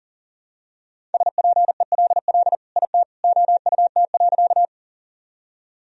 Morse code, heard as a keyed tone, about 40 wpm, 700 Hz. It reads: SPELL IT OUT.
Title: Sentences from Top 1000 Words - Review Part 1 - 40wpm